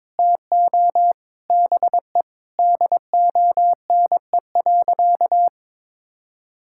Morse code, heard as a keyed tone, about 22 wpm, 700 Hz. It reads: TO BE DONE.
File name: Sentences with 7-Word Phrases - 22wpm